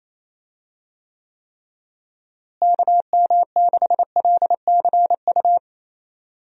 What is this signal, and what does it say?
Morse code, heard as a keyed tone, about 28 wpm, 700 Hz. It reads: KM6LCU
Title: USA Call Signs - 6 Characters - Repeated 3x - 28wpm